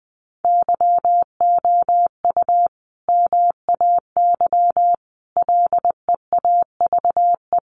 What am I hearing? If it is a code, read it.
Morse code, heard as a keyed tone, about 20 wpm, 700 Hz. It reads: YOU MAY LEAVE
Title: Sentences from Top 400 Words 20wpm